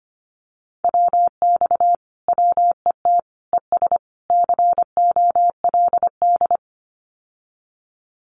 Morse code, heard as a keyed tone, about 25 wpm, 700 Hz. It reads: WX WET ES COLD
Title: Common QSO Phrases Encore 25wpm